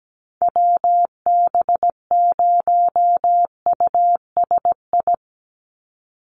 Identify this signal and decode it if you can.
Morse code, heard as a keyed tone, about 17 wpm, 700 Hz. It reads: WB0USI
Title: NA SOTA Chaser Callsigns 17wpm